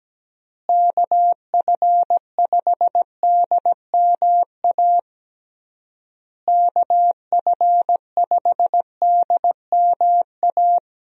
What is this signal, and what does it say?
Morse code, heard as a keyed tone, about 17 wpm, 700 Hz. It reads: KF5DMA KF5DMA
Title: USA Call Signs - 6 Characters - Repeated 3x - 17wpm